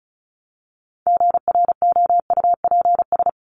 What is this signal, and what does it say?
Morse code, heard as a keyed tone, about 34 wpm, 700 Hz. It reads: GROUPS
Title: ES:S Suffix - Top 100 Words with Base Words 35wpm